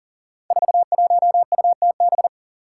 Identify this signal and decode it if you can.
Morse code, heard as a keyed tone, about 42 wpm, 700 Hz. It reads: 41UTB
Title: Five Challenging Letter-Number Combinations 40wpm